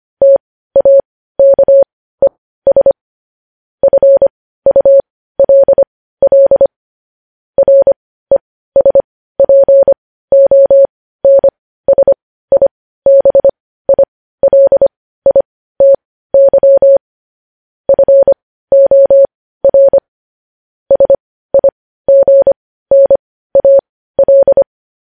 Morse code, heard as a keyed tone, about 25 wpm, spaced as for 15 wpm, 550 Hz. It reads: TAKES FULL RESPONSIBILITY FOR SIGNAL